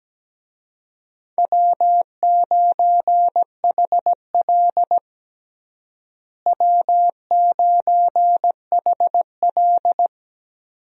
Morse code, heard as a keyed tone, about 17 wpm, 700 Hz. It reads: W9HL W9HL
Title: USA Call Signs - 4 Characters - Repeated 2x - 17wpm